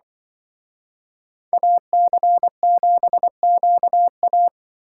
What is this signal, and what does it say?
Morse code, heard as a keyed tone, about 25 wpm, 700 Hz. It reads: AC7QA
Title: Speed Racing - USA 2x2 Call Signs 20wpm